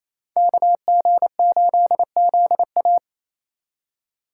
Morse code, heard as a keyed tone, about 32 wpm, 700 Hz. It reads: KG8ZA
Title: Speed Racing - USA 2x2 Call Signs 28wpm